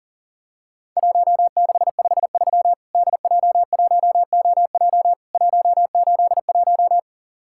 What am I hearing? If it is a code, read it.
Morse code, heard as a keyed tone, about 37 wpm, 700 Hz. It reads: 1653 DJ1OJ 181
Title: Worked All Europe DX Contest - QTC Exchange - 40wpm